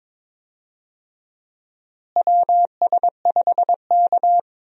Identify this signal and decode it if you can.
Morse code, heard as a keyed tone, about 22 wpm, 700 Hz. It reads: WS5K
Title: USA Call Signs - 4 Characters - Repeated 2x - 22wpm